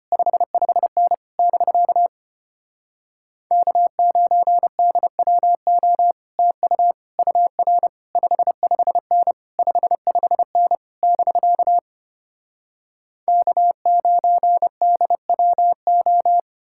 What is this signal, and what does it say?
Morse code, heard as a keyed tone, about 33 wpm, 700 Hz. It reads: 55N <BK> K9DWO TU UR 55N 55N <BK> K9DWO